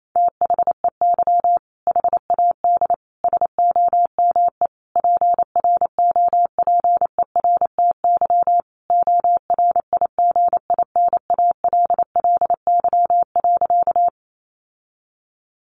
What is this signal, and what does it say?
Morse code, heard as a keyed tone, about 28 wpm, 700 Hz. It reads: THEY HAD SOME PROPERTY ORIGINALLY.
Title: Sentences from Top 1500 Words 28wpm